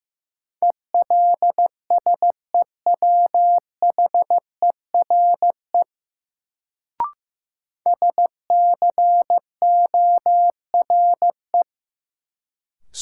Morse code, heard as a keyed tone, about 15 wpm, 700 Hz. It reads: ELSEWHERE SCORE
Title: Sentences from Top 1600 Words - New Word Drill - 15wpm